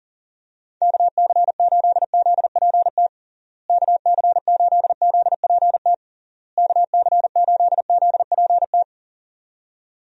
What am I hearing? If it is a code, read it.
Morse code, heard as a keyed tone, about 40 wpm, 700 Hz. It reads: KC8ZPT KC8ZPT KC8ZPT
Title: USA Call Signs - 6 Characters - Repeated 3x - 40wpm